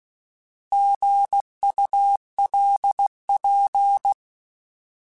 Morse code, heard as a keyed tone, about 16 wpm, 800 Hz. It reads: GULP
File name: Alphabetical / Alphabetical mp3s / Gs @15